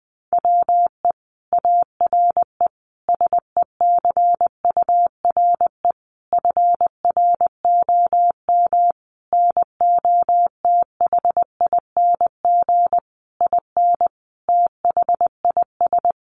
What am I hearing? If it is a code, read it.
Morse code, heard as a keyed tone, about 20 wpm, 700 Hz. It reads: WE ARE SECURE FROM NOTHING IN THIS
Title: Sentences from Top 1200 Words 20wpm